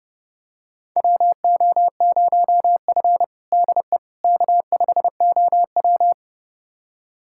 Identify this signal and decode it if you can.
Morse code, heard as a keyed tone, about 31 wpm, 700 Hz. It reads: WO0F DE K5OW
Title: USA Callsign DE Callsign - 4 Characters - 30wpm